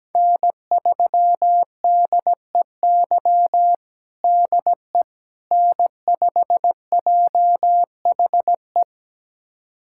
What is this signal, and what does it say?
Morse code, heard as a keyed tone, about 17 wpm, 700 Hz. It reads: N3DEY DE N5JHE
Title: USA Callsign DE Callsign - 5 Characters - 17wpm